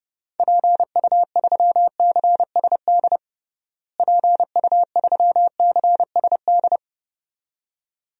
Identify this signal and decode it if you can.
Morse code, heard as a keyed tone, about 30 wpm, 700 Hz. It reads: PU3CSD PU3CSD
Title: DXCC-100 Call Signs - 6 Characters - Repeated 2x - 30wpm